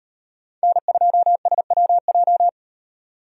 Morse code, heard as a keyed tone, about 36 wpm, 700 Hz. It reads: N2SWJ